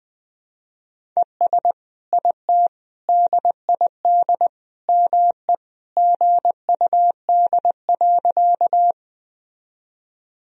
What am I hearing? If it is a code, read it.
Morse code, heard as a keyed tone, about 21 wpm, 700 Hz. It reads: ES IT DID ME GUD.
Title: Sentences from Top 200 Words and Abbreviations 20wpm